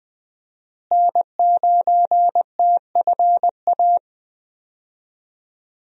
Morse code, heard as a keyed tone, about 21 wpm, 700 Hz. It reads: N9TFA
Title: USA Call Signs - 5 Characters - Repeated 3x - 20wpm